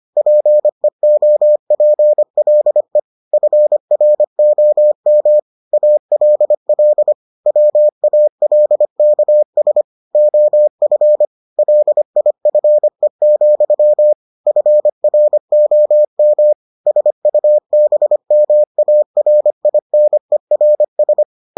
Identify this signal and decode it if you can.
Morse code, heard as a keyed tone, about 25 wpm, 600 Hz. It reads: PEOPLE FROM ALL WALKS OF LIFE, FROM SUBMARINERS